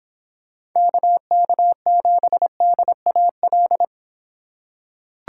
Morse code, heard as a keyed tone, about 26 wpm, 700 Hz. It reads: KK7DAL